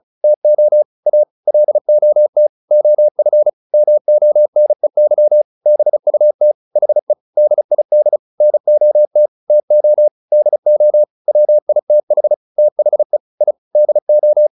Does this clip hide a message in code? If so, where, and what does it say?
Yes: Morse code from 0.2 to 14.5 s, heard as a keyed tone, about 34 wpm, 600 Hz. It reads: TO A LOT OF MONEY BUT HE DID NOT TO DO WITH THE I DO